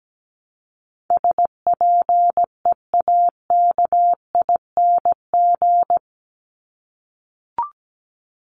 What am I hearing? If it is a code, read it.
Morse code, heard as a keyed tone, about 17 wpm, 700 Hz. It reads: SPEAKING
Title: ING Suffix - Top 100 Words with Base Words 17wpm